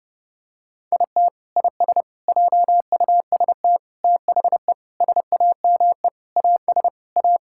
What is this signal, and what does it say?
Morse code, heard as a keyed tone, about 32 wpm, 700 Hz. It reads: IT IS JUST THE SAME AS A
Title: Sentences from Top 1000 Words - Review Part 6 - 30wpm